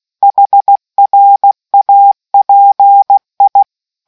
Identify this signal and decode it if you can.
Morse code, heard as a keyed tone, about 16 wpm, 800 Hz. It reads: HRAPI